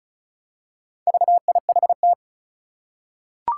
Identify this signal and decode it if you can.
Morse code, heard as a keyed tone, about 36 wpm, 700 Hz. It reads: VIHT
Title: Four Challenging Letter-Number Combinations 35wpm